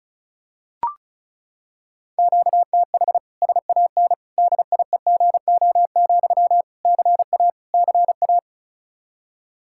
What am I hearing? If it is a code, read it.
Morse code, heard as a keyed tone, about 34 wpm, 700 Hz. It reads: QTH SAN DIEGO, CA CA
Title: Practice QSO - Exchange 1 - Beginning - 35wpm